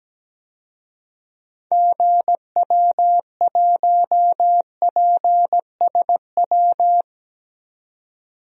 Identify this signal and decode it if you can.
Morse code, heard as a keyed tone, about 17 wpm, 700 Hz. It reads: GW1PSW
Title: DXCC-100 Call Signs - 6 Characters - 17wpm